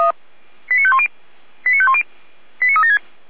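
Five-tone selective call (ZVEI), heard as quick runs of five tones, tones of about 80 ms. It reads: 89510 89510 89376
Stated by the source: ZVEI five-tone selective call 89207 received by SDR.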